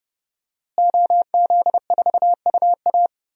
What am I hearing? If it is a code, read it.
Morse code, heard as a keyed tone, about 31 wpm, 700 Hz. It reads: OZ4UA
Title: DXCC-100 Call Signs - 5 Characters - Repeated 2x - 30wpm